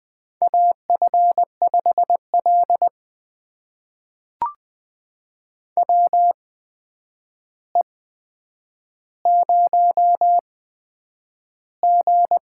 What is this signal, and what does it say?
Morse code, heard as a keyed tone, about 20 wpm, spaced as for 9 wpm, 700 Hz. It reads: AF5L WE0G
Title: USA Call Signs - 4 Characters - 8x Character Spacing - 20wpm